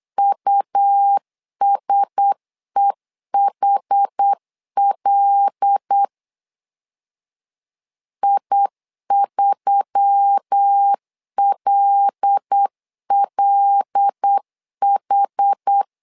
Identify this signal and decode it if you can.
Morse code, heard as a keyed tone, about 8 wpm, 800 Hz. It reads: USEHL I3LLH
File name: LEZIONI / 5 LBVC / 5 L 8wpm 2831